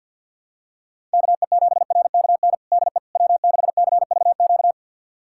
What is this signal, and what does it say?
Morse code, heard as a keyed tone, about 48 wpm, 700 Hz. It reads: KE7RKN DE W6CVQ
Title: USA Callsign DE Callsign 50wpm